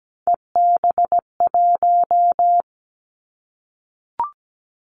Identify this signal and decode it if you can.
Morse code, heard as a keyed tone, about 17 wpm, 700 Hz. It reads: EB1
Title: Callsign Prefixes for Top 25 DXCC - 17wpm